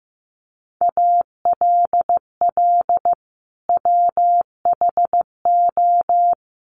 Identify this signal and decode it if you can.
Morse code, heard as a keyed tone, about 15 wpm, 700 Hz. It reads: ALL WHO
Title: Sentences from Top 1400 Words 15wpm